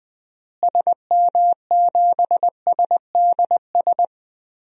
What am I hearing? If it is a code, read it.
Morse code, heard as a keyed tone, about 20 wpm, 700 Hz. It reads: SM7SDS
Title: DXCC-100 Call Signs - 6 Characters - Repeated 3x - 20wpm